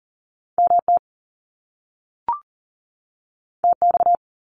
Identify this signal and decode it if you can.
Morse code, heard as a keyed tone, about 42 wpm, 700 Hz. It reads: MT TX
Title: US States 40wpm